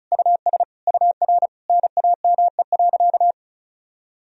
Morse code, heard as a keyed tone, about 36 wpm, 700 Hz. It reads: US UR NAME.
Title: Sentences from Top 200 Words and Abbreviations 35wpm